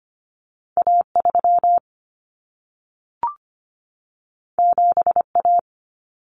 Morse code, heard as a keyed tone, about 25 wpm, 700 Hz. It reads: A3 7A
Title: Common Cut Numbers - 2-Digit 25wpm